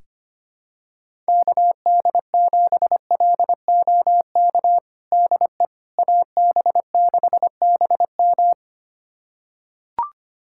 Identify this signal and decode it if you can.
Morse code, heard as a keyed tone, about 25 wpm, 700 Hz. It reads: KD7LOK DE AB6BM
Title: USA Callsign DE Callsign 25wpm